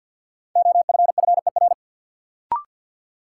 Morse code, heard as a keyed tone, about 50 wpm, 700 Hz. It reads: OFFER
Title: Sentences from Top 900 Words - New Word Drill - 50wpm